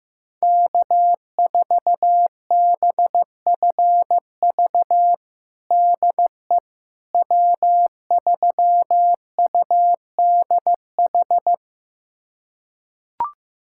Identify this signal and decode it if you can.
Morse code, heard as a keyed tone, about 15 wpm, 700 Hz. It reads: K4BFV DE W3UDH
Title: USA Callsign DE Callsign - 5 Characters - 15wpm